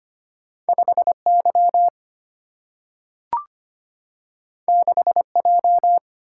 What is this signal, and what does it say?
Morse code, heard as a keyed tone, about 25 wpm, 700 Hz. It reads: 5Y 6J
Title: Single-Letter Single-Number 25wpm